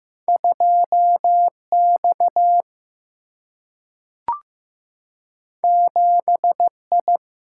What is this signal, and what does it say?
Morse code, heard as a keyed tone, about 15 wpm, 700 Hz. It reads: 2 X 7 I
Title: Single-Letter Single-Number 15wpm